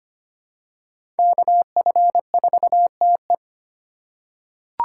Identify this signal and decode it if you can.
Morse code, heard as a keyed tone, about 25 wpm, 700 Hz. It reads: KF4TE
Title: USA Call Signs - 5 Characters - 3x Character Spacing - 25wpm